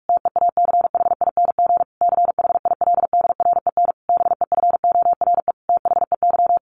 Morse code, heard as a keyed tone, about 44 wpm, 700 Hz. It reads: TEACHING CHILDREN BEFORE THEY
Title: Sentences from Top 1100 Words 45wpm